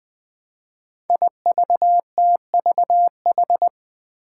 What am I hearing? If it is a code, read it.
Morse code, heard as a keyed tone, about 20 wpm, 700 Hz. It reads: IVTVH
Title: Five Challenging Letter-Number Combinations 20wpm